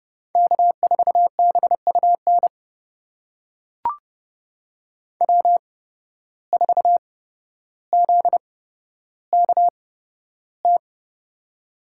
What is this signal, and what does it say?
Morse code, heard as a keyed tone, about 30 wpm, spaced as for 13 wpm, 700 Hz. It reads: K4BUN W4ZKT
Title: USA Call Signs - 5 Characters - 8x Character Spacing - 30wpm